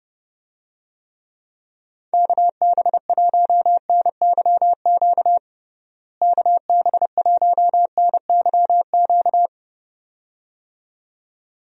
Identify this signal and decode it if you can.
Morse code, heard as a keyed tone, about 30 wpm, 700 Hz. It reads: KB1NYQ KB1NYQ
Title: USA Call Signs - 6 Characters - Repeated 2x - 30wpm